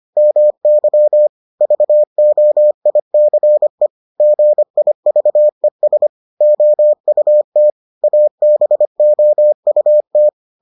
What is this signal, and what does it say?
Morse code, heard as a keyed tone, about 25 wpm, 600 Hz. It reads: MY VOICE GIVES OUT ABOUT